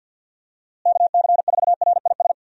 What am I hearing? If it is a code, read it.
Morse code, heard as a keyed tone, about 53 wpm, 700 Hz. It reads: KC4RIS
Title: USA Call Signs - 6 Characters - Repeated 3x - 50wpm